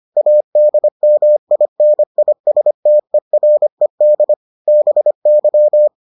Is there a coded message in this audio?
Yes: Morse code from 0.2 to 5.9 s, heard as a keyed tone, about 25 wpm, 600 Hz. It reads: ADMINISTERED BY